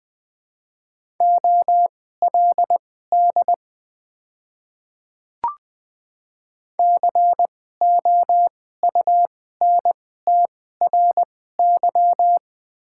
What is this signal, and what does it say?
Morse code, heard as a keyed tone, about 20 wpm, spaced as for 14 wpm, 700 Hz. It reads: OLD COUNTRY
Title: 2x Character Spacing - Top 300 Words - 20wpm